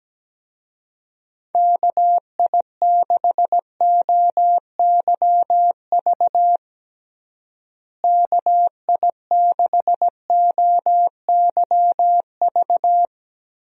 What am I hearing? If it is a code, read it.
Morse code, heard as a keyed tone, about 17 wpm, 700 Hz. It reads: KI6OYV KI6OYV
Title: USA Call Signs - 6 Characters - Repeated 2x - 17wpm